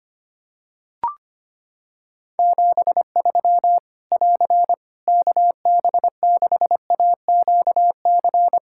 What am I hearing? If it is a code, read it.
Morse code, heard as a keyed tone, about 25 wpm, 700 Hz. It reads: 73 <AR> KB6AQC
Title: Practice QSO - Exchange 4 - Ending - 25wpm